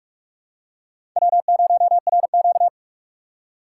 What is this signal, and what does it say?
Morse code, heard as a keyed tone, about 46 wpm, 700 Hz. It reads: W0RQ
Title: USA Call Signs - 4 Characters - 4x Character Spacing - 45wpm